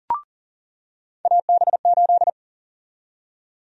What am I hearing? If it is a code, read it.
Morse code, heard as a keyed tone, about 37 wpm, 700 Hz. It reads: AB8
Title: USA Call Sign Prefixes - 40wpm